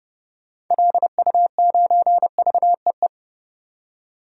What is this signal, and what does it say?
Morse code, heard as a keyed tone, about 30 wpm, 700 Hz. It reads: LU9VEE
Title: DXCC-100 Call Signs - 6 Characters - 30wpm